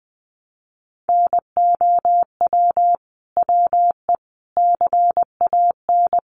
Morse code, heard as a keyed tone, about 20 wpm, 700 Hz. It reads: NOW WE CAN
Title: Sentences from Top 700 Words 20wpm